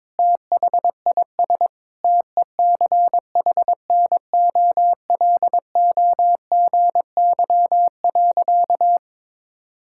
Morse code, heard as a keyed tone, about 21 wpm, 700 Hz. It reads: THIS TECHNOLOGY.
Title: Sentences from Top 1000 Words - Review Part 5 - 22wpm